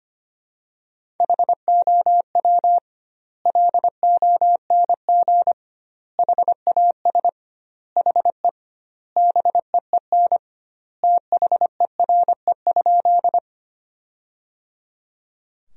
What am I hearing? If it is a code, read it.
Morse code, heard as a keyed tone, about 25 wpm, 700 Hz. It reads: HOW LONG HAS HE BEEN THERE?
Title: 2x Word-Spacing - Sentences from Top 100 Words - 25wpm